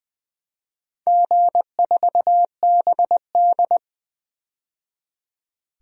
Morse code, heard as a keyed tone, about 20 wpm, 700 Hz. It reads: G4BD
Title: DXCC-100 Call Signs - 4 Characters - 20wpm